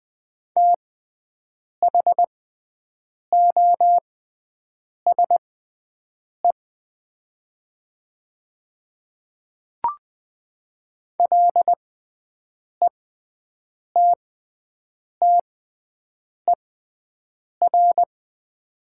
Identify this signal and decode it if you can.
Morse code, heard as a keyed tone, about 20 wpm, spaced as for 7 wpm, 700 Hz. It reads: THOSE LETTER
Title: 6x Character Spacing - Top 300 Words - 20wpm